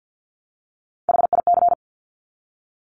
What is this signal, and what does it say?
Morse code, heard as a keyed tone, about 50 wpm, 700 Hz. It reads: HIC
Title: Top Three Letter Combinations in Words 50wpm